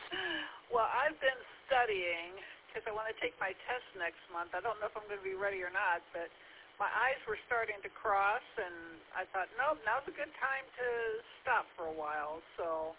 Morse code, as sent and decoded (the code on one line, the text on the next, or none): none